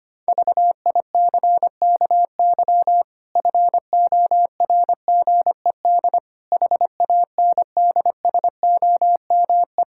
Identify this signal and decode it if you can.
Morse code, heard as a keyed tone, about 25 wpm, 700 Hz. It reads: VICKY FORGED HANDSOME
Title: Pangram Challenge v2 - 25wpm